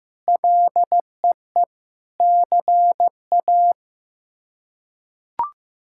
Morse code, heard as a keyed tone, about 16 wpm, 700 Hz. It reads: LEE CA
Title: K1USN SST QSO Practice - 15wpm